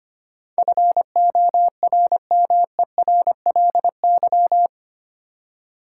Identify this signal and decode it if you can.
Morse code, heard as a keyed tone, about 25 wpm, 690 Hz. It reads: FORMERLY
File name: 8 Letter Words - First 3 Letters Given - 25wpm